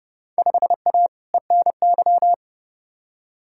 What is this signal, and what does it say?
Morse code, heard as a keyed tone, about 30 wpm, 700 Hz. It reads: 5A ENY
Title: Field Day Exchanges - 30wpm